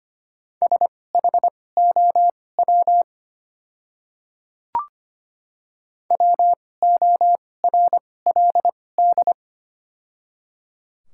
Morse code, heard as a keyed tone, about 25 wpm, spaced as for 18 wpm, 690 Hz. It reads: SHOW WORLD
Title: 2x Character Spacing - Top 200 Words - 25wpm